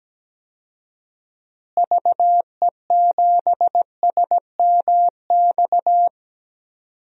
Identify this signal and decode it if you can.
Morse code, heard as a keyed tone, about 17 wpm, 700 Hz. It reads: VE7SMX
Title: DXCC-100 Call Signs - 6 Characters - Repeated 2x - 17wpm